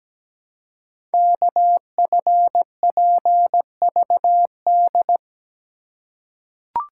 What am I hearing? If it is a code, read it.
Morse code, heard as a keyed tone, about 17 wpm, 700 Hz. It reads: KFPVD
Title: Code Groups - 5 Letters - 17wpm